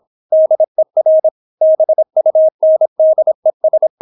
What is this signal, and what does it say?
Morse code, heard as a keyed tone, about 26 wpm, 630 Hz. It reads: DER BUNDES